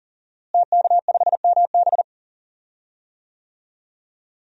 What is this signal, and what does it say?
Morse code, heard as a keyed tone, about 40 wpm, 700 Hz. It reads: TK5MB
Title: DXCC-100 Call Signs - 5 Characters - Repeated 3x - 40wpm